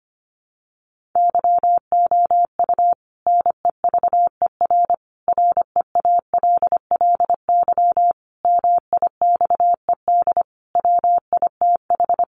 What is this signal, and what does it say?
Morse code, heard as a keyed tone, about 25 wpm, 700 Hz. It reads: YOU NEVER REALLY MIXED WITH
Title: Sentences from Top 1500 Words 25wpm